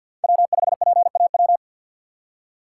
Morse code, heard as a keyed tone, about 48 wpm, 690 Hz. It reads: W5PAW
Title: USA Call Signs - 5 Characters - 6x Character Spacing - 50wpm